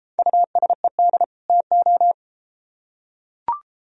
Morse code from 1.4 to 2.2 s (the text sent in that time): TO